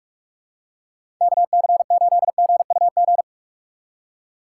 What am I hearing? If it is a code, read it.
Morse code, heard as a keyed tone, about 46 wpm, 700 Hz. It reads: KC8GUG
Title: USA Call Signs - 6 Characters - 3x Character Spacing - 45wpm